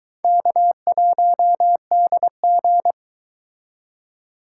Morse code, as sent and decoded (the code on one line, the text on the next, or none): -.- .---- -.. --.
K1DG